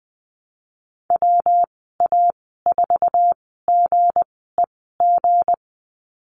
Morse code, heard as a keyed tone, about 20 wpm, spaced as for 15 wpm, 700 Hz. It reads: WA4GEG
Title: 167 Lesson 27g - Number 7 - Call Signs 20wpm